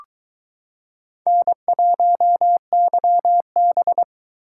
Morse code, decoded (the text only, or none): N1YB